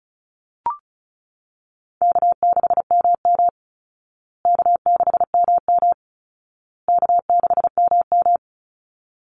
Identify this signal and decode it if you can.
Morse code, heard as a keyed tone, about 34 wpm, 700 Hz. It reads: K6MM K6MM K6MM
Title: Call Signs Repeated 2x - 35wpm